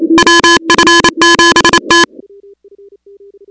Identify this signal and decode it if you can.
Morse code, heard as a keyed tone, about 28 wpm, 1700 Hz, with noise. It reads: WF7T
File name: contests / 2019-cw-wpx-kt5j-k5pi- -2330 0-16